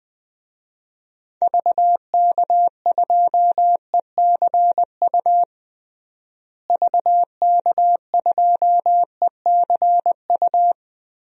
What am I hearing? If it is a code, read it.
Morse code, heard as a keyed tone, about 21 wpm, 700 Hz. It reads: VK2ECU VK2ECU
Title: DXCC-100 Call Signs - 6 Characters - Repeated 3x - 20wpm